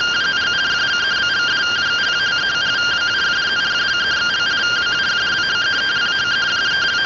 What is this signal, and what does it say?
Two-tone teleprinter signal, about 170 Hz shift.